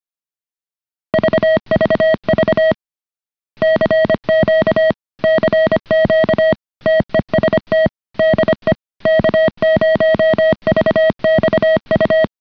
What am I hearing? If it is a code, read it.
Morse code, heard as a keyed tone, about 25 wpm, 630 Hz. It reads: VVV CQ CQ TEST DE K0VXU